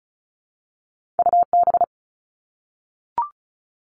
Morse code, heard as a keyed tone, about 36 wpm, 700 Hz. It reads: UB
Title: All Cut Numbers - 2-Digit 35wpm